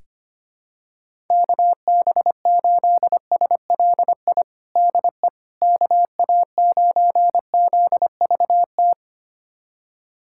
Morse code, heard as a keyed tone, about 25 wpm, 700 Hz. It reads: KB8SLI DE KA9ZVT
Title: USA Callsign DE Callsign - 6 Characters - 25wpm